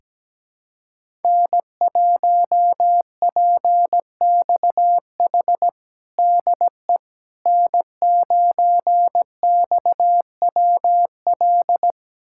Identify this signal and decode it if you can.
Morse code, heard as a keyed tone, about 17 wpm, 700 Hz. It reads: N1PXH DE N9XWL
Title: USA Callsign DE Callsign - 5 Characters - 17wpm